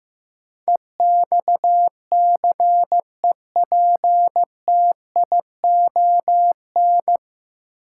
Morse code, heard as a keyed tone, about 15 wpm, 700 Hz. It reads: EXCEPTION
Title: Sets of 4 Words - New Word Drill - 15wpm